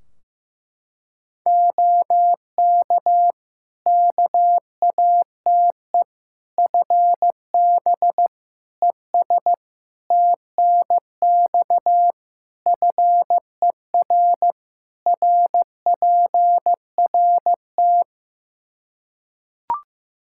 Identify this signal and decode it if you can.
Morse code, heard as a keyed tone, about 16 wpm, 700 Hz. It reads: OK KATE FB ES TNX FER RPRT
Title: Practice QSO - Exchange 2 - Info - 15wpm